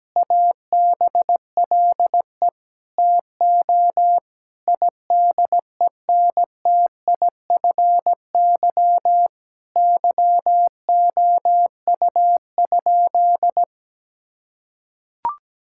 Morse code, heard as a keyed tone, about 17 wpm, 700 Hz. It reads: ABLE TO IDENTIFY YOU?
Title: Sentences from Top 1000 Words - Review Part 6 - 17wpm